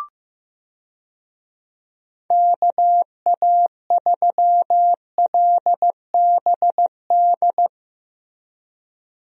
Morse code, heard as a keyed tone, about 15 wpm, 700 Hz. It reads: KA3LBD
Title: USA Call Signs - 6 Characters - Repeated 2x - 15wpm